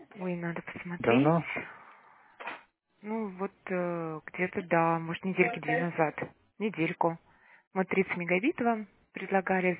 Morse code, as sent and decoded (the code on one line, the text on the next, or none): none